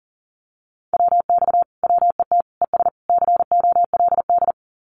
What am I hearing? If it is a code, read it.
Morse code, heard as a keyed tone, about 40 wpm, 700 Hz. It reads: WX WET ES COLD
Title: Common QSO Phrases Encore 40wpm